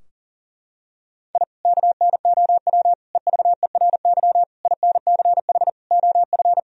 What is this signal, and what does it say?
Morse code, heard as a keyed tone, about 40 wpm, 700 Hz. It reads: I KNOW EVERY INCH OF
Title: Sentences from Top 1000 Words - Review Part 3 - 40wpm